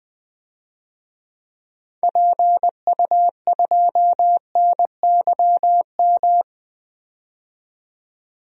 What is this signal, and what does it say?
Morse code, heard as a keyed tone, about 20 wpm, 700 Hz. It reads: PU2NYM
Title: DXCC-100 Call Signs - 6 Characters - 20wpm